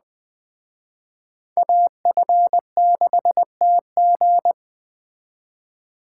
Morse code, heard as a keyed tone, about 20 wpm, 700 Hz. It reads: AF6TG